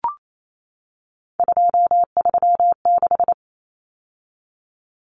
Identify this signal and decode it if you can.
Morse code, heard as a keyed tone, about 28 wpm, 700 Hz. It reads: 236
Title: Numbers - 3-Digit 28wpm